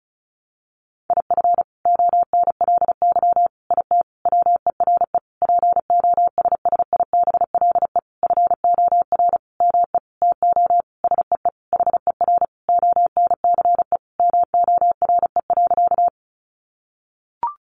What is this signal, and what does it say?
Morse code, heard as a keyed tone, about 34 wpm, 700 Hz. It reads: IF ONLY IT WERE POSSIBLE FOR ME TO SEE HER ONCE MORE.
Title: Sentences from Top 500 Words 35wpm